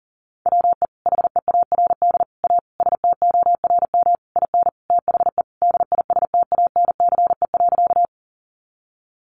Morse code, heard as a keyed tone, about 40 wpm, 700 Hz. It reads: WE HEARD A STORM IN THE DISTANCE.
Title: Sentences from Top 1500 Words 40wpm